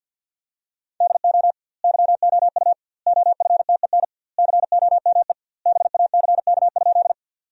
Morse code, heard as a keyed tone, about 48 wpm, 700 Hz. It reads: DO YOU OFTEN COME BACK?
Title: 2x Word-Spacing - Sentences from Top 300 Words - 50wpm